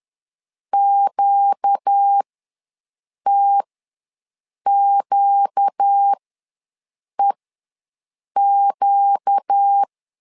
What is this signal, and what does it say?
Morse code, heard as a keyed tone, about 10 wpm, spaced as for 6 wpm, 800 Hz. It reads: QTQEQ
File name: LEZIONI / 4 Q ws 10wpm 2306